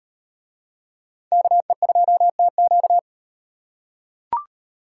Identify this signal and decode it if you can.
Morse code, heard as a keyed tone, about 38 wpm, 700 Hz. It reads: KE2TQ